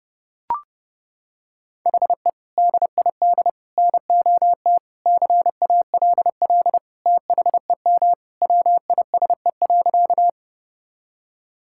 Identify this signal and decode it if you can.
Morse code, heard as a keyed tone, about 32 wpm, 700 Hz. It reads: HE DID NOT CALL THEM WISE.
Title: Sentences from Top 1600 Words 30wpm